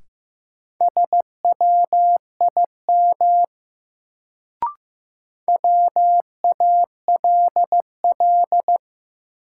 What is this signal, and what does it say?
Morse code, heard as a keyed tone, about 15 wpm, 700 Hz. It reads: SWIM WALL
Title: Sentences from Top 1000 Words - Review Part 6 - New Word Drill - 15wpm